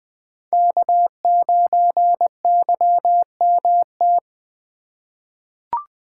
Morse code, heard as a keyed tone, about 21 wpm, 700 Hz. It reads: K9YMT
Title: USA Call Signs - 5 Characters - 6x Character Spacing - 20wpm